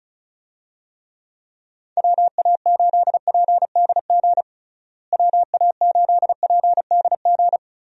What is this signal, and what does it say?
Morse code, heard as a keyed tone, about 34 wpm, 700 Hz. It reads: WA8PDG WA8PDG
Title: USA Call Signs - 6 Characters - Repeated 3x - 35wpm